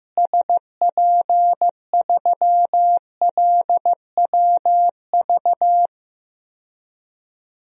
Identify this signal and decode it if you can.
Morse code, heard as a keyed tone, about 15 wpm, 700 Hz. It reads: SP3LWV
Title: DXCC-100 Call Signs - 6 Characters - Repeated 2x - 15wpm